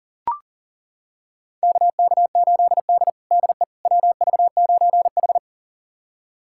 Morse code, heard as a keyed tone, about 40 wpm, 700 Hz. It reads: KK8D DE WV9H
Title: USA Callsign DE Callsign - 4 Characters - 40wpm